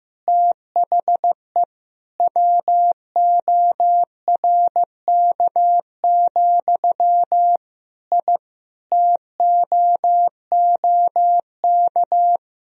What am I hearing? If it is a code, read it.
Morse code, heard as a keyed tone, about 15 wpm, 700 Hz. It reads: THE WORK, I TOOK
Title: Sentences from Top 1000 Words - Review Part 4 - 15wpm